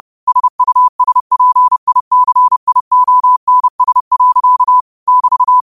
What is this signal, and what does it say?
Morse code, heard as a keyed tone, about 30 wpm, 1000 Hz. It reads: SUSPICIONS. <BT>
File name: qotdfiles / 30wpm